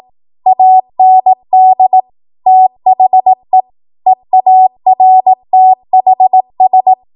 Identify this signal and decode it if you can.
Morse code, heard as a keyed tone, about 18 wpm, 750 Hz. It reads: AND THE EARTHS
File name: Morse / 18WPM